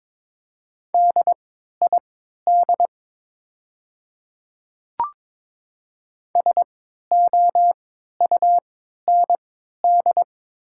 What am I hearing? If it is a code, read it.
Morse code, heard as a keyed tone, about 22 wpm, spaced as for 12 wpm, 700 Hz. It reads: DID SOUND
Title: 3x Character Spacing - Top 100 Words - 22wpm